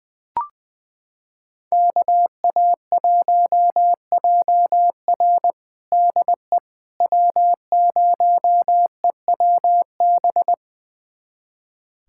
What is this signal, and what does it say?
Morse code, heard as a keyed tone, about 21 wpm, 700 Hz. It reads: KA1JR DE W0EWB
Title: USA Callsign DE Callsign - 5 Characters - 20wpm